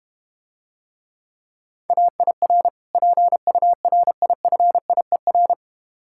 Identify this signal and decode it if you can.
Morse code, heard as a keyed tone, about 30 wpm, 700 Hz. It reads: AIR PURIFIER
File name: practice / 25wpm / Difficult Words - Speed-Racing with Hints - 25wpm